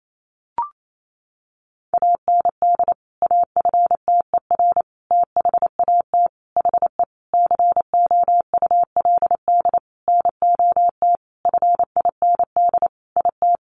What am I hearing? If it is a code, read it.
Morse code, heard as a keyed tone, about 28 wpm, 700 Hz. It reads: AND AFTER THAT HE COULD NOT FIND IT
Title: Sentences from Top 1400 Words 28wpm